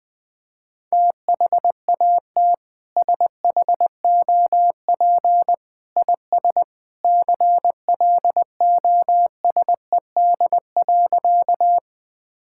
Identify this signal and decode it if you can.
Morse code, heard as a keyed tone, about 20 wpm, 700 Hz. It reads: THAT SHOP IS CLOSED.
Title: Sentences from Top 1600 Words 20wpm